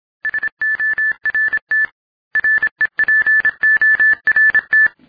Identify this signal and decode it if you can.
Morse code, heard as a keyed tone, about 26 wpm, 1750 Hz. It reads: SORT REPORT